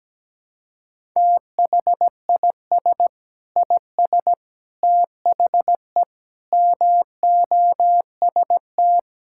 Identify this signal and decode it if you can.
Morse code, heard as a keyed tone, about 17 wpm, 700 Hz. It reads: THIS IS THE MOST
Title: Sentences from Top 1000 Words - Review Part 6 - 17wpm